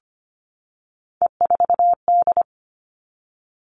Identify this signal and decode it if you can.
Morse code, heard as a keyed tone, about 25 wpm, 700 Hz. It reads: E4D
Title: All Cut Numbers - 3-Digit 25wpm